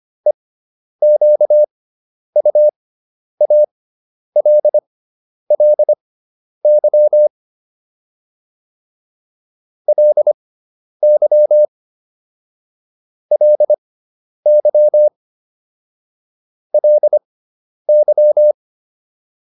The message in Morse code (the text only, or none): EQUALLY LY LY LY